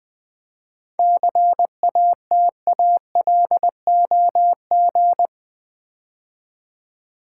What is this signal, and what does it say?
Morse code, heard as a keyed tone, about 20 wpm, 700 Hz. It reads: CATALOG
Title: 7 Letter Words - First 3 Letters Given - 20wpm